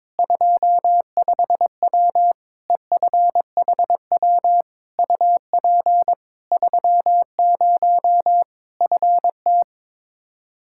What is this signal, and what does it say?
Morse code, heard as a keyed tone, about 22 wpm, 700 Hz. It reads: 25W EFHW UP 30 FT
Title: Practice QSO - Exchange 2 - Info - 22wpm